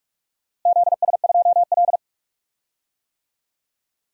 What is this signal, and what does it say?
Morse code, heard as a keyed tone, about 42 wpm, 700 Hz. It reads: ZS2L
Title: DXCC-100 Call Signs - 4 Characters - 45wpm